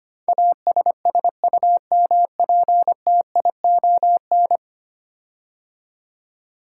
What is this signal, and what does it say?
Morse code, heard as a keyed tone, about 25 wpm, 700 Hz. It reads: ASSUMPTION
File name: 10 Letter Words - First 3 Letters Given - 25wpm